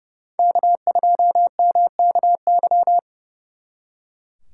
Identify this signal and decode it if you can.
Morse code, heard as a keyed tone, about 38 wpm, 700 Hz. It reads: K2MKY